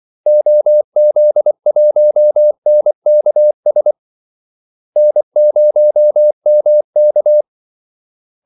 Morse code, heard as a keyed tone, about 24 wpm, 600 Hz. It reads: OZ1NKS N0MK
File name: Callsigns-24wpm